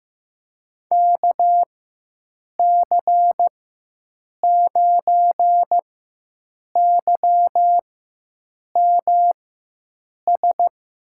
Morse code, heard as a keyed tone, about 15 wpm, spaced as for 7 wpm, 700 Hz. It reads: KC9YMS
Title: USA Call Signs - 6 Characters - 4x Character Spacing - 15wpm